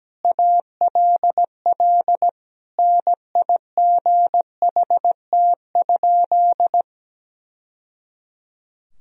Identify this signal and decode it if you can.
Morse code, heard as a keyed tone, about 17 wpm, 700 Hz. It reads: ALL NIGHT?
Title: Sentences from Top 1000 Words - Review Part 4 - 17wpm